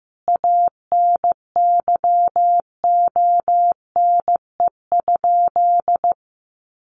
Morse code, heard as a keyed tone, about 15 wpm, 700 Hz. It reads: ANYONE?
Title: Sentences from Top 1400 Words 15wpm